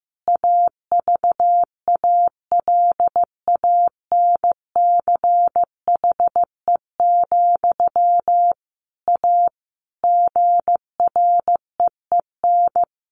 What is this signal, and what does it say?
Morse code, heard as a keyed tone, about 15 wpm, 700 Hz. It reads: AVALANCHE, A GREEN
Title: A Sound Of Thunder 15wpm